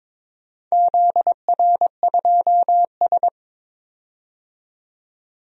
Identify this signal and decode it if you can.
Morse code, heard as a keyed tone, about 22 wpm, 700 Hz. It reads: ZR2S
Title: DXCC-100 Call Signs - 4 Characters - 22wpm